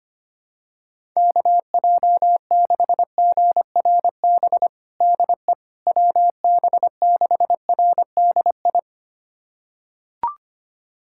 Morse code, heard as a keyed tone, about 25 wpm, 700 Hz. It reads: KJ6GRB DE WB6RDI
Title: USA Callsign DE Callsign - 6 Characters - 25wpm